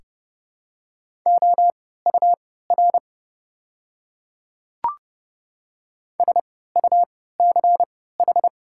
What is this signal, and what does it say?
Morse code, heard as a keyed tone, about 30 wpm, spaced as for 17 wpm, 700 Hz. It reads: OUR SUCH
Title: 3x Character Spacing - Top 200 Words - 30wpm